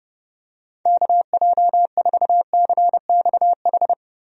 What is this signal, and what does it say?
Morse code, heard as a keyed tone, about 32 wpm, 700 Hz. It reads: KJ4CXH
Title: USA Call Signs - 6 Characters - Repeated 3x - 30wpm